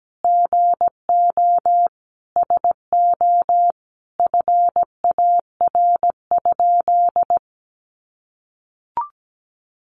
Morse code, heard as a keyed tone, about 17 wpm, 700 Hz. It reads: GO SO FAR?
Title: Sentences from Top 300 Words 17wpm